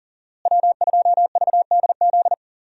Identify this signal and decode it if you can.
Morse code, heard as a keyed tone, about 40 wpm, 700 Hz. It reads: W2VDZ
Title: USA Call Signs - 5 Characters - Repeated 2x - 40wpm